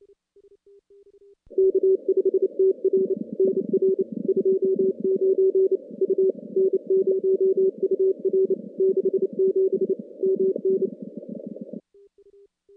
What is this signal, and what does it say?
Morse code, heard as a keyed tone, about 28 wpm, 380 Hz. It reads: K5TR NR 29 U N0UR 67 MN
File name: contests / CW-SS-k5tr- -2100